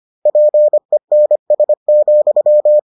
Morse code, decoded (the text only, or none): PENS,